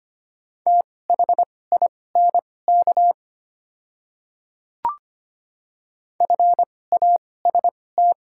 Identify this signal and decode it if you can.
Morse code, heard as a keyed tone, about 25 wpm, spaced as for 18 wpm, 700 Hz. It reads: THINK FAST
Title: 2x Character Spacing - Top 500 Words - 25wpm